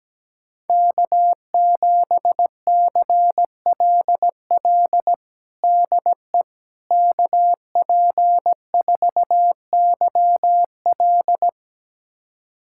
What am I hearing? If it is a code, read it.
Morse code, heard as a keyed tone, about 17 wpm, 700 Hz. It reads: K7CLL DE KP4YL
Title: USA Callsign DE Callsign - 5 Characters - 17wpm